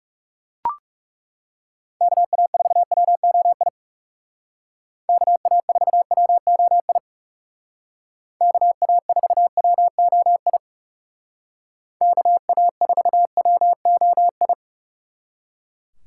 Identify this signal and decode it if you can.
Morse code, heard as a keyed tone, about 39 wpm, 700 Hz. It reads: KA4WOI KA4WOI KA4WOI KA4WOI